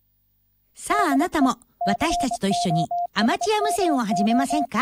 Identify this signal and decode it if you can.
Morse code, heard as a keyed tone, about 24 wpm, 700 Hz, with speech. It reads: DE COSMO